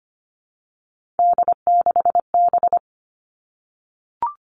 Morse code, heard as a keyed tone, about 25 wpm, 700 Hz. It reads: D6B
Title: Three Challenging Letter-Number Combinations 25wpm